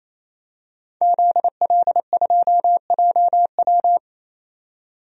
Morse code, heard as a keyed tone, about 28 wpm, 700 Hz. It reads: ZL2JW
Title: DXCC-100 Call Signs - 5 Characters - 28wpm